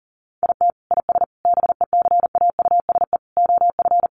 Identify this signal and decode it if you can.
Morse code, heard as a keyed tone, about 40 wpm, 700 Hz. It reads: IT IS BECAUSE OF
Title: Sentences from Top 600 Words 40wpm